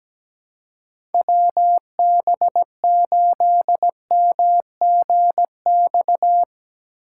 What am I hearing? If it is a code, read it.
Morse code, heard as a keyed tone, about 17 wpm, 700 Hz. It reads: WB8MGX
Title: USA Call Signs - 6 Characters - 2x Character Spacing - 17wpm